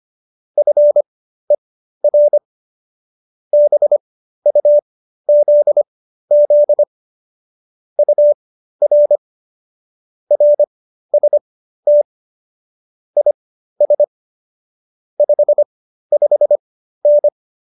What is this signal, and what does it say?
Morse code, heard as a keyed tone, about 25 wpm, spaced as for 13 wpm, 590 Hz. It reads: FER BUZZ UR RST IS 55N